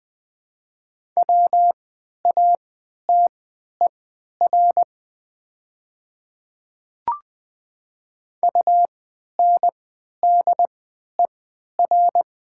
Morse code, heard as a keyed tone, about 20 wpm, spaced as for 11 wpm, 700 Hz. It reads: WATER UNDER
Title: 3x Character Spacing - Top 300 Words - 20wpm